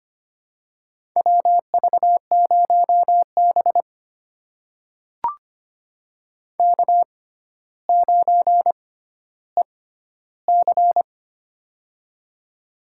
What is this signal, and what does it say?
Morse code, heard as a keyed tone, about 25 wpm, spaced as for 13 wpm, 700 Hz. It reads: WV0B K9EC
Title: USA Call Signs - 4 Characters - 6x Character Spacing - 25wpm